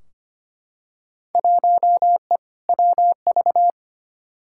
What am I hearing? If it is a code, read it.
Morse code, heard as a keyed tone, about 25 wpm, 700 Hz. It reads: 1E WV
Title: Field Day Exchanges Encore - 25wpm